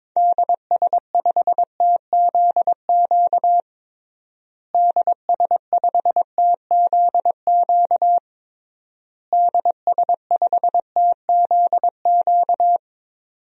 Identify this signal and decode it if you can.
Morse code, heard as a keyed tone, about 22 wpm, 700 Hz. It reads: DS5TZQ DS5TZQ DS5TZQ
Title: DXCC-100 Call Signs - 6 Characters - Repeated 3x - 22wpm